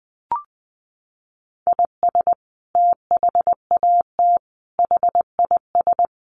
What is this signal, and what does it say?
Morse code, heard as a keyed tone, about 20 wpm, 700 Hz. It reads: IS THAT HIS
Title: Sentences from Top 800 Words 20wpm